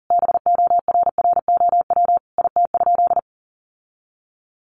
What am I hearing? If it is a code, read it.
Morse code, heard as a keyed tone, about 40 wpm, 700 Hz. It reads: BORROW IT?
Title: Sentences from Top 1500 Words 40wpm